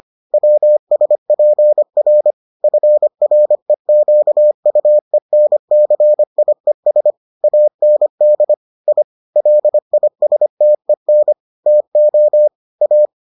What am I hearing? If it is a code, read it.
Morse code, heard as a keyed tone, about 25 wpm, 600 Hz. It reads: WSPR FREQUENCIES AND I LISTEN TO A